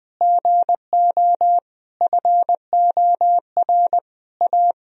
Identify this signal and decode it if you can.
Morse code, heard as a keyed tone, about 20 wpm, 700 Hz. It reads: GO FOR A
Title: Sentences from Top 1000 Words - Review Part 2 - 20wpm